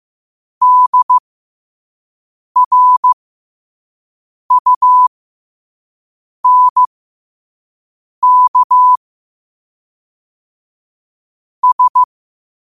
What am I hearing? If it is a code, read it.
Morse code, heard as a keyed tone, about 15 wpm, spaced as for 5 wpm, 1000 Hz. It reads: DRUNK S
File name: qotdfiles / 5wpm